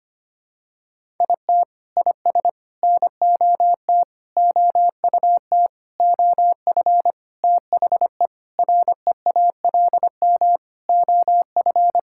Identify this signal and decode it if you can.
Morse code, heard as a keyed tone, about 25 wpm, 700 Hz. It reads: IT IS NOT OUT OF THE REALM OF
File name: practice / 25wpm / Sentences with 9-Word Phrases - 25wpm